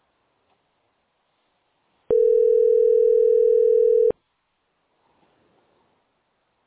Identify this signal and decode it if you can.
Ringback tone (North America), 440 + 480 Hz.